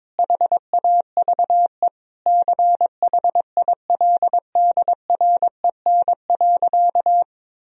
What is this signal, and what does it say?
Morse code, heard as a keyed tone, about 22 wpm, 700 Hz. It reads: HAVE CHILDREN.
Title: Sentences from Top 1000 Words - Review Part 1 - 22wpm